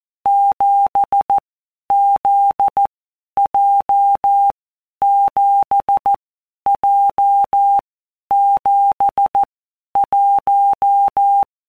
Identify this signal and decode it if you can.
Morse code, heard as a keyed tone, about 14 wpm, 800 Hz. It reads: 7 Z J 7 J 7 1
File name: Basic Letter Groups By WPM's / Groups@13 / JZ17@13